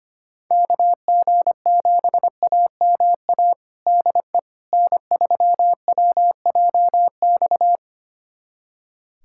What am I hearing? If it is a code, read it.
Morse code, heard as a keyed tone, about 25 wpm, 700 Hz. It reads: KG7AMA DE N3WJX
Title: USA Callsign DE Callsign 25wpm